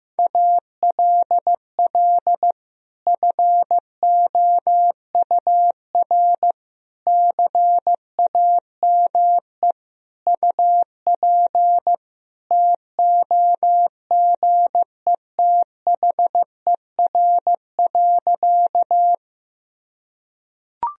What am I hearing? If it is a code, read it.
Morse code, heard as a keyed tone, about 15 wpm, 700 Hz. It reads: ALL FOUR CAME UP TOGETHER.
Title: Sentences from Top 1000 Words - Review Part 1 - 15wpm